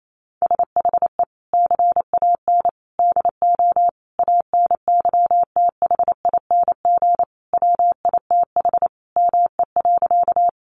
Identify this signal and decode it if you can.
Morse code, heard as a keyed tone, about 28 wpm, 700 Hz. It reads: SHE CAN DO ANYTHING WITH ME.
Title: Sentences from Top 1000 Words 28wpm